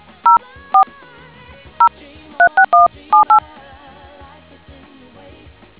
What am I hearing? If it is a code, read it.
DTMF, heard as telephone touch tones, tones of about 110 ms. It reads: 04 036178